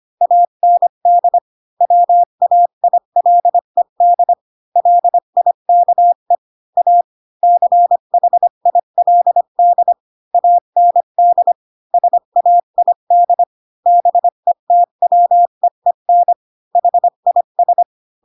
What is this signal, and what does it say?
Morse code, heard as a keyed tone, about 25 wpm, 700 Hz. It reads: AND WAILED LIKE A CHILD AND SAID BETWEEN HIS
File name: Pinocchio / Pinocchio ch3 20 25 ik1hge